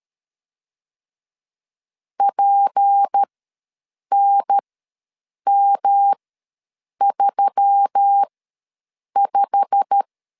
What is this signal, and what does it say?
Morse code, heard as a keyed tone, about 13 wpm, spaced as for 7 wpm, 800 Hz. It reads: PNM35
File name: LEZIONI / 9 2867 ws 12wpm 3431